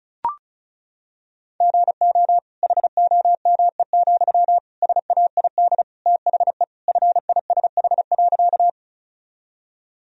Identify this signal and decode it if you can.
Morse code, heard as a keyed tone, about 34 wpm, 700 Hz. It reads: GO HOME, SAID THE FISH.
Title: Sentences from Top 1000 Words - Review Part 1 - 35wpm